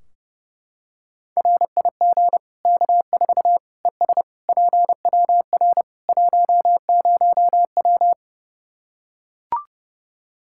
Morse code, heard as a keyed tone, about 32 wpm, 700 Hz. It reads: RIG K4 ES PWR 10W
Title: Practice QSO - Exchange 2 - Info - 30wpm